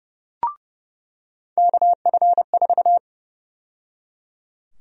Morse code, heard as a keyed tone, about 30 wpm, 700 Hz. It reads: KF4
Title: USA Call Sign Prefixes - 30wpm